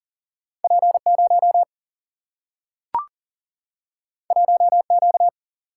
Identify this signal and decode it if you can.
Morse code, heard as a keyed tone, about 40 wpm, 700 Hz. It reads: P0 1Q
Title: Single-Letter Single-Number 40wpm